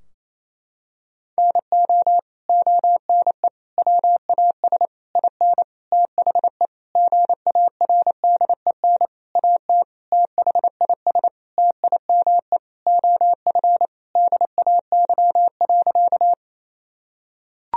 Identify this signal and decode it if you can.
Morse code, heard as a keyed tone, about 28 wpm, 700 Hz. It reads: NO ONE WAS IN THE GARDEN AT THIS TIME OF DAY.
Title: Sentences from Top 1000 Words - Review Part 4 - 28wpm